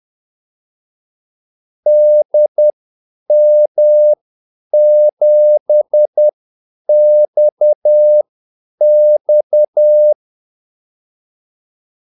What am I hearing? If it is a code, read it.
Morse code, heard as a keyed tone, about 10 wpm, spaced as for 8 wpm, 600 Hz. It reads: DM7XX
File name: Callsigns-10-8wpm